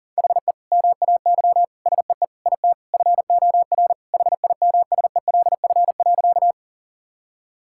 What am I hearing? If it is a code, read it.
Morse code, heard as a keyed tone, about 37 wpm, 700 Hz. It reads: HE MAY SEE IT FOR HIMSELF.
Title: Sentences from Top 1600 Words 40wpm